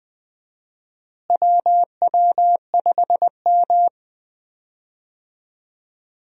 Morse code, heard as a keyed tone, about 20 wpm, 700 Hz. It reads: WW5M
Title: USA Call Signs - 4 Characters - Repeated 2x - 20wpm